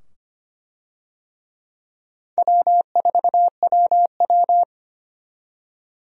Morse code, heard as a keyed tone, about 25 wpm, 700 Hz. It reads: W4WW
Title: USA Call Signs - 4 Characters - 25wpm